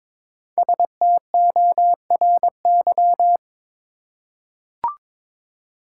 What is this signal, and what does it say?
Morse code, heard as a keyed tone, about 22 wpm, 700 Hz. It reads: STORY
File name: 5 Letter Words - First 3 Letters Given - 22wpm